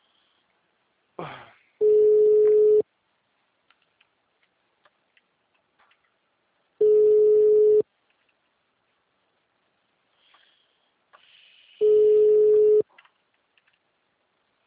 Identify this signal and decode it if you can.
Ringback tone (Europe, 425 Hz).